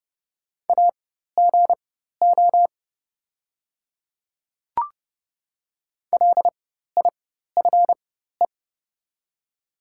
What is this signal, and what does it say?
Morse code, heard as a keyed tone, about 30 wpm, spaced as for 14 wpm, 700 Hz. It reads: AGO LIFE
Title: 4x Character Spacing - Top 500 Words - 30wpm